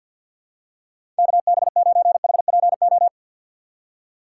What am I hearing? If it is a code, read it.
Morse code, heard as a keyed tone, about 52 wpm, 700 Hz. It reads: KB9HPO
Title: USA Call Signs - 6x Character Spacing - 50wpm